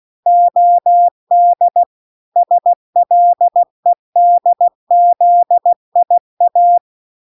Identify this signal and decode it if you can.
Morse code, heard as a keyed tone, about 16 wpm, 700 Hz. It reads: OD SLEDZIA